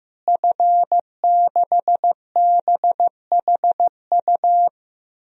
Morse code, heard as a keyed tone, about 15 wpm, 700 Hz. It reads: F6BHU
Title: DXCC-100 Call Signs - 5 Characters - Repeated 2x - 15wpm